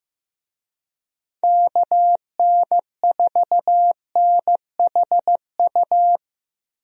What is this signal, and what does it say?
Morse code, heard as a keyed tone, about 15 wpm, 700 Hz. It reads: KN4NHU
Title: USA Call Signs - 6 Characters - Repeated 2x - 15wpm